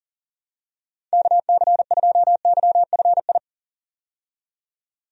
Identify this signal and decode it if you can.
Morse code, heard as a keyed tone, about 39 wpm, 700 Hz. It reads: KC2YFI
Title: USA Call Signs - 6 Characters - 40wpm